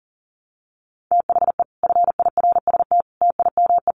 Morse code, heard as a keyed tone, about 40 wpm, 700 Hz. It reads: THE FIRST TIME
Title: Sets of 5 Words Encore 40wpm